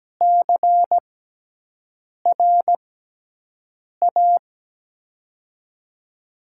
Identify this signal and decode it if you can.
Morse code, heard as a keyed tone, about 17 wpm, spaced as for 6 wpm, 700 Hz. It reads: CRA
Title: USA Call Signs - 6 Characters - 6x Character Spacing - 17wpm